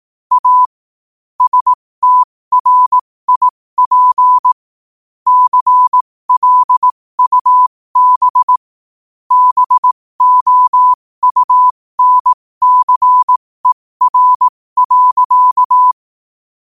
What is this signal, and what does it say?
Morse code, heard as a keyed tone, about 18 wpm, spaced as for 15 wpm, 1000 Hz. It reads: A STRIP CLUB BOUNCER.
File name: qotdfiles / 15wpm